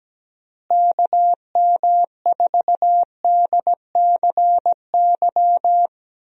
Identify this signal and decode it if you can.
Morse code, heard as a keyed tone, about 17 wpm, 700 Hz. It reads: KM4DCY
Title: USA Call Signs - 6 Characters - Repeated 2x - 17wpm